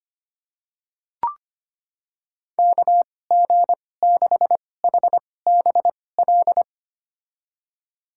Morse code, heard as a keyed tone, about 25 wpm, spaced as for 18 wpm, 700 Hz. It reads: KG6HBL
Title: USA Call Signs - 2x Character Spacing - 25wpm